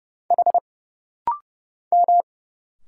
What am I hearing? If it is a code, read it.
Morse code, heard as a keyed tone, about 30 wpm, 700 Hz. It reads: H M
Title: Single Letter-Number Alt - Rapid-Fire 30wpm